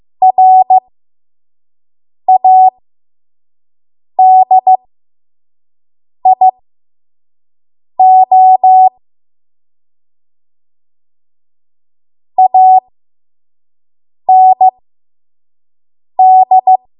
Morse code, heard as a keyed tone, about 15 wpm, spaced as for 5 wpm, 750 Hz. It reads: RADIO AND
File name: Morse / 05WPM